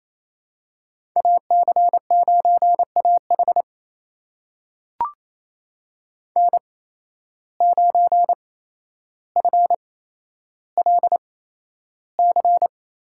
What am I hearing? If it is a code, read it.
Morse code, heard as a keyed tone, about 28 wpm, spaced as for 12 wpm, 700 Hz. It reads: AC9AH N9FLC
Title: USA Call Signs - 5 Characters - 8x Character Spacing - 28wpm